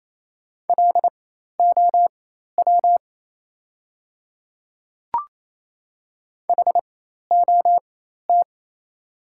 Morse code, heard as a keyed tone, about 27 wpm, spaced as for 13 wpm, 700 Hz. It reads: LOW HOT
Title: 4x Character Spacing - Top 500 Words - 28wpm